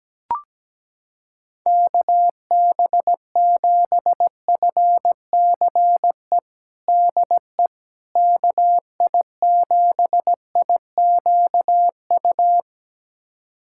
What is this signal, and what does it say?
Morse code, heard as a keyed tone, about 17 wpm, 700 Hz. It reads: KB7FCE DE KI7IQU
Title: USA Callsign DE Callsign - 6 Characters - 17wpm